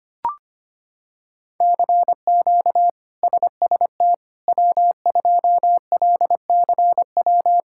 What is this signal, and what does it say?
Morse code, heard as a keyed tone, about 25 wpm, 700 Hz. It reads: CQ SST W2LCW
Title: K1USN SST QSO Practice - 25wpm